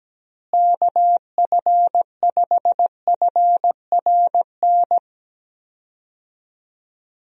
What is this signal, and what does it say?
Morse code, heard as a keyed tone, about 17 wpm, 700 Hz. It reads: KF5FRN
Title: USA Call Signs - 6 Characters - Repeated 2x - 17wpm